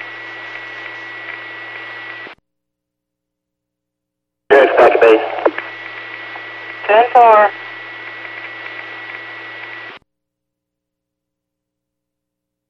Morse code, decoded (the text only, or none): none